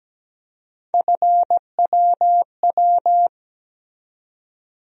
Morse code, heard as a keyed tone, about 17 wpm, 700 Hz. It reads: FWW
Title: USA Call Sign Postfixes - 3 Letters 17wpm